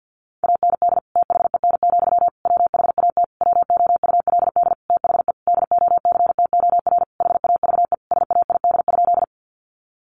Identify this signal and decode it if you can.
Morse code, heard as a keyed tone, about 50 wpm, 700 Hz. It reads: AND THEN, WHAT WOULD THE DOCTOR HAVE SAID?
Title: Sentences from Top 1500 Words 50wpm